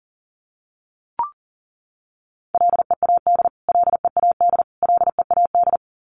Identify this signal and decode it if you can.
Morse code, heard as a keyed tone, about 40 wpm, 700 Hz. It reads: LEAD LEAD LEAD
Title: Top 500 Words Repeated 40wpm